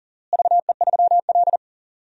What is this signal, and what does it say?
Morse code, heard as a keyed tone, about 40 wpm, 700 Hz. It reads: VE3L
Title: DXCC-100 Call Signs - 4 Characters - Repeated 3x - 40wpm